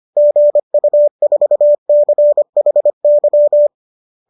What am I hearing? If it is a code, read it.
Morse code, heard as a keyed tone, about 25 wpm, 590 Hz. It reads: GU4CHY
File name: qso210 25